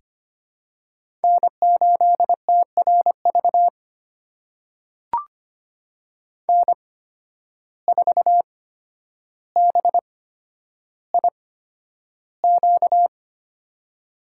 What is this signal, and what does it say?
Morse code, heard as a keyed tone, about 25 wpm, spaced as for 11 wpm, 700 Hz. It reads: N8TRV N4BIQ
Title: USA Call Signs - 5 Characters - 8x Character Spacing - 25wpm